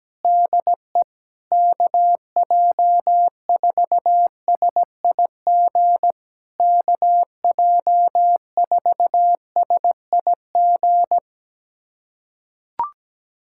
Morse code, heard as a keyed tone, about 17 wpm, 700 Hz. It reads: DE KJ4SIG KJ4SIG
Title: SOTA QSO Practice 17wpm